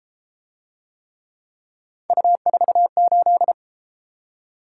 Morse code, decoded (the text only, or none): U48